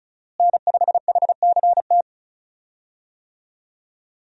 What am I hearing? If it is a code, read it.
Morse code, heard as a keyed tone, about 34 wpm, 700 Hz. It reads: N5HCT
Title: USA Call Signs - 5 Characters - Repeated 3x - 35wpm